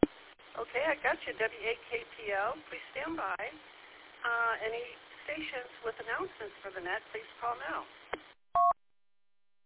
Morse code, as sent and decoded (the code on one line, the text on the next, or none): none